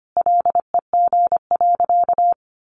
Morse code, heard as a keyed tone, about 25 wpm, 700 Hz. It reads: LEG.
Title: Additional Easy Phrases 25wpm